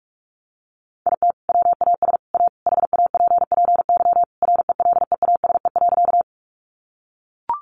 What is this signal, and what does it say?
Morse code, heard as a keyed tone, about 44 wpm, 700 Hz. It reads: IT WAS A HAPPY RELEASE.
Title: Sentences from Top 1100 Words 45wpm